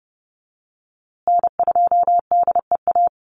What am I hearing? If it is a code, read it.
Morse code, heard as a keyed tone, about 30 wpm, 700 Hz. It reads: N2DEA
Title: USA Call Signs Repeated 3x 30wpm